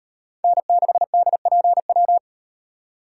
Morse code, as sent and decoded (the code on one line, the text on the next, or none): -. -.... -.. .--. .--
N6DPW